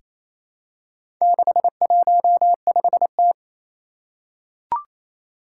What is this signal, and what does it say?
Morse code, heard as a keyed tone, about 27 wpm, 700 Hz. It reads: 615T
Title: Four Challenging Letter-Number Combinations 28wpm